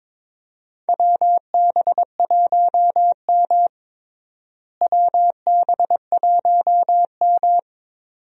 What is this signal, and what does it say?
Morse code, heard as a keyed tone, about 22 wpm, 700 Hz. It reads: WB1M WB1M
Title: USA Call Signs - 4 Characters - Repeated 2x - 22wpm